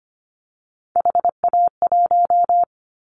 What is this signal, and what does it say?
Morse code, heard as a keyed tone, about 25 wpm, 700 Hz. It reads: HA1
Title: Callsign Prefixes for Top 50 DXCC - 25wpm